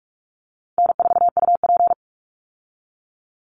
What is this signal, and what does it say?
Morse code, heard as a keyed tone, about 44 wpm, 700 Hz. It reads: N4UP
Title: Call Signs Repeated 2x - 45wpm